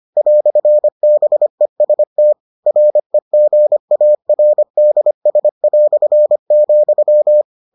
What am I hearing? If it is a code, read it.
Morse code, heard as a keyed tone, about 25 wpm, 600 Hz. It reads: "BEST REGARDS",